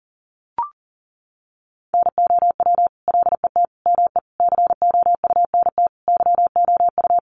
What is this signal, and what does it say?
Morse code, heard as a keyed tone, about 40 wpm, 700 Hz. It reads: NOW LET ME COUNT YOU
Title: Sentences from Top 1500 Words 40wpm